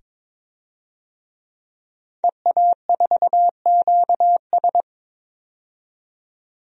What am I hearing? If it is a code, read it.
Morse code, heard as a keyed tone, about 22 wpm, 700 Hz. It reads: EA4QS
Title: DXCC-100 Call Signs - 5 Characters - 22wpm